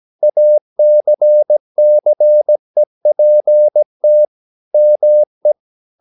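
Morse code, heard as a keyed tone, about 17 wpm, 600 Hz. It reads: ACCEPT ME
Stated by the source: 17 wpm Morse practice sending the English text of a novel.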